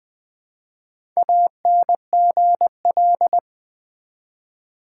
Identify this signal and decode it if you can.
Morse code, heard as a keyed tone, about 20 wpm, 700 Hz. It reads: ANGL
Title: Build A Word - 1 Letter At A Time - 5-Letter Words - 20wpm